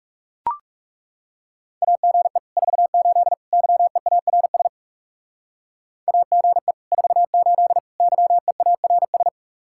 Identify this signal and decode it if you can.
Morse code, heard as a keyed tone, about 36 wpm, 700 Hz. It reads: AGE 48 YEARS AGE 48 YEARS
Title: Speed-Racing Common QSO Phrases - 30wpm